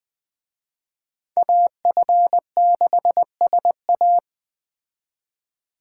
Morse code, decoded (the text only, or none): AF6SA